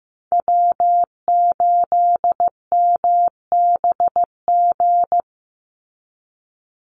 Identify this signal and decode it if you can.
Morse code, heard as a keyed tone, about 15 wpm, 700 Hz. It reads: W8MBG
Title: USA Call Signs 15wpm